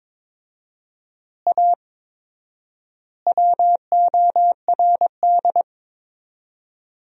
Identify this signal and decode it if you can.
Morse code, heard as a keyed tone, about 22 wpm, 700 Hz. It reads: A WORD
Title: 4x Word-Spacing - Sentences from Top 300 Words - 22wpm